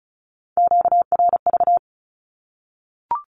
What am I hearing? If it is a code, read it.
Morse code, heard as a keyed tone, about 34 wpm, 700 Hz. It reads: QRV
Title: Q-Codes 35wpm